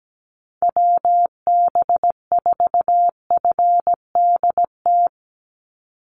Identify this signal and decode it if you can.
Morse code, heard as a keyed tone, about 17 wpm, 700 Hz. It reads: WB4FDT
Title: Call Signs Repeated 2x - 17wpm